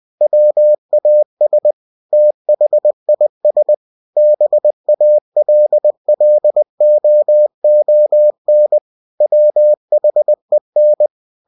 Morse code, heard as a keyed tone, about 20 wpm, 600 Hz. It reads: WAS THIS BALLOON WHEN
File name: cw-books / winnie-the-pooh / winnie-20wpm-page-0056